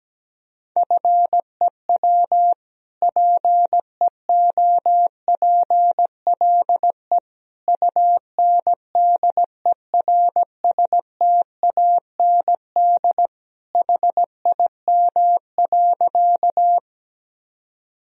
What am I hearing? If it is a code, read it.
Morse code, heard as a keyed tone, about 17 wpm, 700 Hz. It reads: FEW PEOPLE UNDERSTAND HIM.
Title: Sentences from Top 1000 Words - Review Part 1 - 17wpm